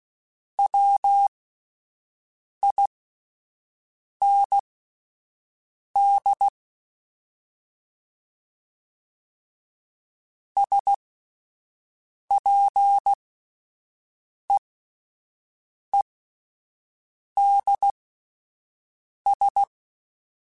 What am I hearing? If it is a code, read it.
Morse code, heard as a keyed tone, about 16 wpm, spaced as for 5 wpm, 800 Hz. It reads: WIND SPEEDS
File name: Spaced Characters / Spaced Characters-mp3s / Weather4 21 sp @10